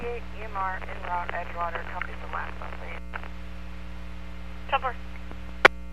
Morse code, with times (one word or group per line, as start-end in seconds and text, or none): none